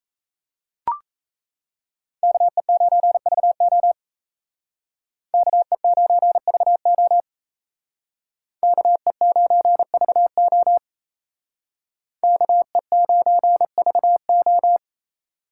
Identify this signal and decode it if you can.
Morse code, heard as a keyed tone, about 35 wpm, 700 Hz. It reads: KE9VO KE9VO KE9VO KE9VO